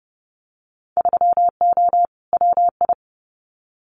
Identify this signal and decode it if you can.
Morse code, heard as a keyed tone, about 31 wpm, 700 Hz. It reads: 3O WI
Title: Winter Field Day Exchanges 30wpm